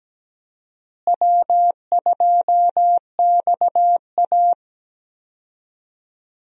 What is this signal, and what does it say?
Morse code, heard as a keyed tone, about 17 wpm, 700 Hz. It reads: W2XA
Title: USA Call Signs - 4 Characters - Repeated 2x - 17wpm